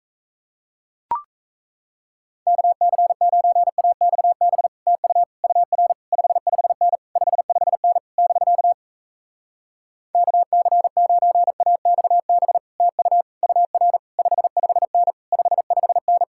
Morse code, heard as a keyed tone, about 37 wpm, 700 Hz. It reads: KC9AXB TU UR 55N 55N <BK> KC9AXB TU UR 55N 55N